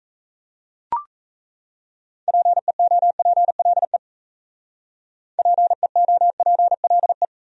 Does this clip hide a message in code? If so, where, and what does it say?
Yes: Morse code from 2.3 to 7.3 s, heard as a keyed tone, about 37 wpm, 700 Hz. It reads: PEOPLE PEOPLE